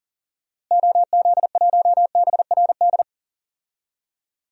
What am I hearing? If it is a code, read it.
Morse code, heard as a keyed tone, about 40 wpm, 700 Hz. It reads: OZ1BRD
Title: DXCC-100 Call Signs - 6 Characters - Repeated 3x - 40wpm